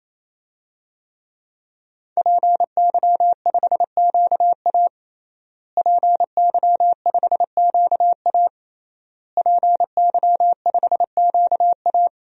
Morse code, heard as a keyed tone, about 28 wpm, 700 Hz. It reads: PY5QA PY5QA PY5QA
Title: DXCC-100 Call Signs - 5 Characters - Repeated 3x - 28wpm